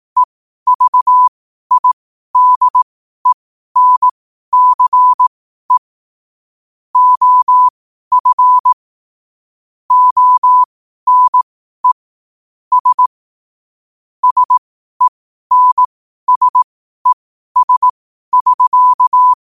Morse code, heard as a keyed tone, about 18 wpm, spaced as for 12 wpm, 1000 Hz. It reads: EVIDENCE OF ONE S SENSES<SK>
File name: qotdfiles / 12wpm